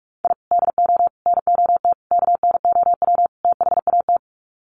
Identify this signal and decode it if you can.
Morse code, heard as a keyed tone, about 44 wpm, 700 Hz. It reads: I DO NOT KNOW THAT
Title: Sets of 5 Words 45wpm